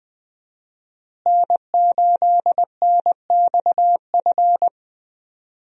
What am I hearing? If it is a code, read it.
Morse code, heard as a keyed tone, about 20 wpm, 700 Hz. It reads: N8NXF
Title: USA Call Signs - 5 Characters - Repeated 2x - 20wpm